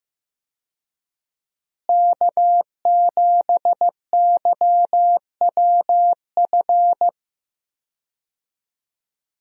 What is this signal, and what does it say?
Morse code, heard as a keyed tone, about 15 wpm, 700 Hz. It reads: K7YWF
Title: USA Call Signs - 5 Characters - Repeated 2x - 15wpm